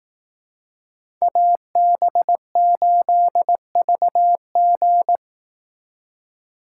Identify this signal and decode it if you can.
Morse code, heard as a keyed tone, about 18 wpm, 700 Hz. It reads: AB8VG
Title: Speed Racing - USA 2x2 Call Signs 15wpm